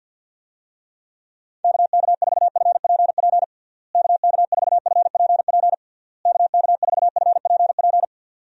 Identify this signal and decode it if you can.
Morse code, heard as a keyed tone, about 50 wpm, 700 Hz. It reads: KK4FPP KK4FPP KK4FPP
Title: USA Call Signs - 6 Characters - Repeated 3x - 50wpm